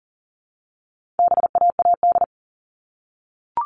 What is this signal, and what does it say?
Morse code, heard as a keyed tone, about 40 wpm, 700 Hz. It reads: BAAD
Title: All Cut Numbers - 4-Digit 40wpm